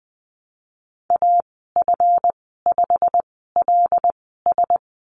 Morse code, heard as a keyed tone, about 20 wpm, spaced as for 15 wpm, 700 Hz. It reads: AF5LS
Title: 231 Lesson 37g - Letter x - Call Signs 20wpm